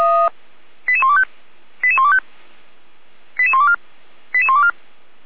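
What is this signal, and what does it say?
Five-tone selective call (ZVEI), heard as quick runs of five tones, tones of about 70 ms. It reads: 88125 88125 88124 88124
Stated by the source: ZVEI five-tone call received by SDR, from the Feuerwehr Orlamünde (fire brigade) server audio.f85360.feuerwehr-orlamuende.org.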